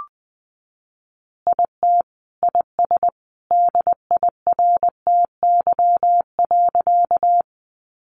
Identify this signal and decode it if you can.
Morse code, heard as a keyed tone, about 20 wpm, 700 Hz. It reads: IT IS DIRTY.
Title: Additional Easy Phrases 20wpm